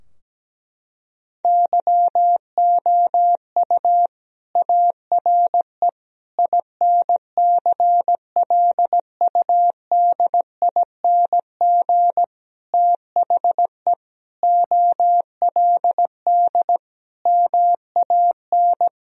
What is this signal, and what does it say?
Morse code, heard as a keyed tone, about 17 wpm, 700 Hz. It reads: YOU ARE INCLUDING THE OLD MAN
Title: Sentences from Top 1000 Words - Review Part 4 - 17wpm